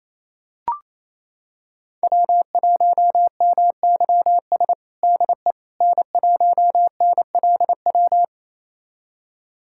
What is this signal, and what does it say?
Morse code, heard as a keyed tone, about 27 wpm, 700 Hz. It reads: W1MYS DE N1NLW
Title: USA Callsign DE Callsign - 5 Characters - 28wpm